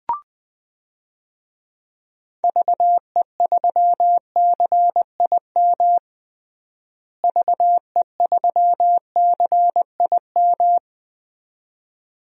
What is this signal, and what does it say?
Morse code, heard as a keyed tone, about 20 wpm, 700 Hz. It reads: VE3CIM VE3CIM
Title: DXCC-100 Call Signs - 6 Characters - Repeated 2x - 20wpm